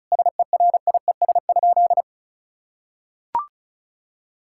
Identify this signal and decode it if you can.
Morse code, heard as a keyed tone, about 34 wpm, 700 Hz. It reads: SERIES?
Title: Sentences from Top 1000 Words - Review Part 5 - 35wpm